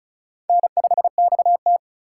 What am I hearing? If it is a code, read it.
Morse code, heard as a keyed tone, about 43 wpm, 700 Hz. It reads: N5XT
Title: Speed Racing - USA 1x2 Call Signs 35wpm